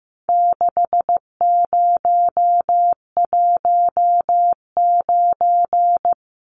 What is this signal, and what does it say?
Morse code, heard as a keyed tone, about 15 wpm, 700 Hz. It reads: 6019
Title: Numbers - 4-Digit 15wpm